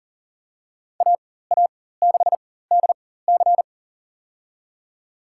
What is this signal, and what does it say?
Morse code, heard as a keyed tone, about 40 wpm, spaced as for 19 wpm, 700 Hz. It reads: AA6DC
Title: USA Call Signs - 5 Characters - 4x Character Spacing - 40wpm